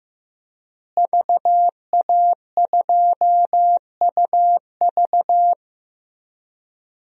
Morse code, heard as a keyed tone, about 16 wpm, 700 Hz. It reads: VA2UV
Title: DXCC-100 Call Signs - 5 Characters - Repeated 2x - 15wpm